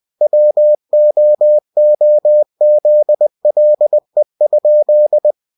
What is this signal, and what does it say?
Morse code, heard as a keyed tone, about 20 wpm, 600 Hz. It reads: WOOZLE?
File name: winnie-20wpm